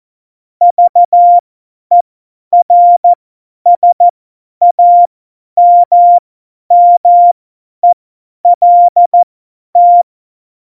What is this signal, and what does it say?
Morse code, heard as a keyed tone, about 14 wpm, spaced as for 10 wpm, 700 Hz. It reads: VERSAMMELT
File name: Morsekurs / Witze / Witz 12. Januar 2021, 50 F70